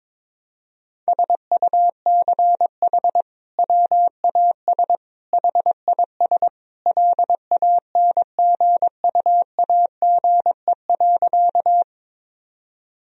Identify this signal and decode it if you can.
Morse code, heard as a keyed tone, about 22 wpm, 700 Hz. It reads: SUCH WAS HIS LANGUAGE.
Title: Sentences from Top 1000 Words - Review Part 3 - 22wpm